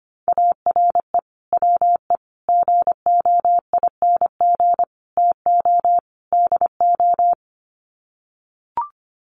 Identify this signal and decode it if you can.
Morse code, heard as a keyed tone, about 25 wpm, 700 Hz. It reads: ARE WE GOING TO DO
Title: Sets of 5 Words 25wpm